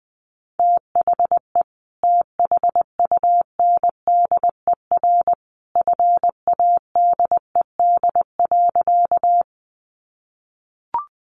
Morse code, heard as a keyed tone, about 20 wpm, 700 Hz. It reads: THE THUNDER FADED.
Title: A Sound Of Thunder 20wpm